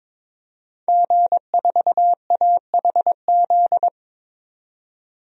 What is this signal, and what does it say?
Morse code, heard as a keyed tone, about 22 wpm, 700 Hz. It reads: G4AHZ
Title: DXCC-100 Call Signs - 5 Characters - 22wpm